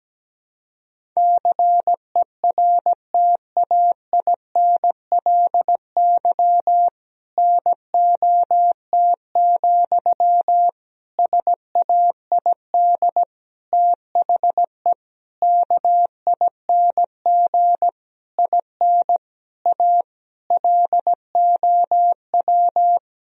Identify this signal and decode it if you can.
Morse code, heard as a keyed tone, about 17 wpm, 700 Hz. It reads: CERTAINLY NOT, SAID THE KING IN A LOW
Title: Sentences from Top 1000 Words - Review Part 1 - 17wpm